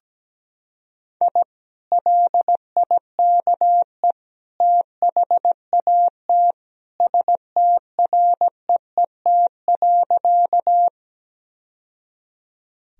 Morse code, heard as a keyed tone, about 17 wpm, 700 Hz. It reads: I LIKE THAT STREET.
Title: Sentences from Top 1000 Words - Review Part 3 - 17wpm